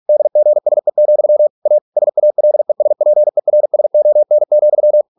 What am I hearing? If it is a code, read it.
Morse code, heard as a keyed tone, about 46 wpm, 600 Hz. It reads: DGSE, A SALESPERSON,